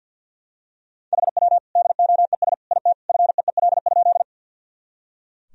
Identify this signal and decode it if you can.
Morse code, heard as a keyed tone, about 45 wpm, 700 Hz. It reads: HW DOES IT FEEL?
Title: Sentences from Top 400 Words and Abbreviations 50wpm